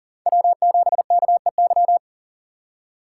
Morse code, heard as a keyed tone, about 40 wpm, 700 Hz. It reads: W7KEY
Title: USA Call Signs - 5 Characters - 2x Character Spacing - 40wpm